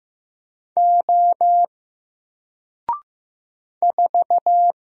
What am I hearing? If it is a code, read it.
Morse code, heard as a keyed tone, about 15 wpm, 700 Hz. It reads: O 4
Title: Single Letter-Number Alt 15wpm